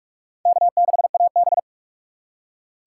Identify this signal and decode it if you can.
Morse code, heard as a keyed tone, about 43 wpm, 700 Hz. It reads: K6AB